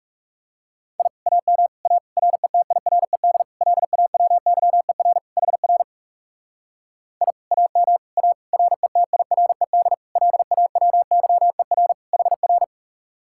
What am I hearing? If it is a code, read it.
Morse code, heard as a keyed tone, about 36 wpm, 700 Hz. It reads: I AM A RETIRED LAWYER HR I AM A RETIRED LAWYER HR
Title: Speed-Racing Common QSO Phrases Encore - 30wpm